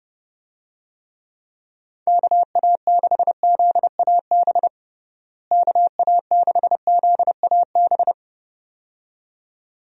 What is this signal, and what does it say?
Morse code, heard as a keyed tone, about 30 wpm, 700 Hz. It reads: KA6ZAB KA6ZAB
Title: USA Call Signs - 6 Characters - Repeated 2x - 30wpm